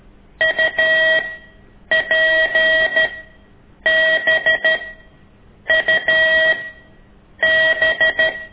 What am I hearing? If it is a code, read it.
Morse code, heard as a keyed tone, about 13 wpm, spaced as for 7 wpm, 1900 Hz. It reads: UPBUB
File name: ik1weg2 / CORSI CW